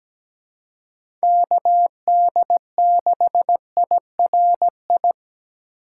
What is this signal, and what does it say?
Morse code, heard as a keyed tone, about 17 wpm, 700 Hz. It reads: KD6IRI
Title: K1USN SST QSO Practice - 17wpm